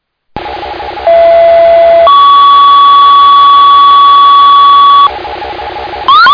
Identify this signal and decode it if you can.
Two-tone page, heard as a one-second tone, then a three-second tone, about 670 then 1130 Hz. It is heard from 1.1 to 5.1 s.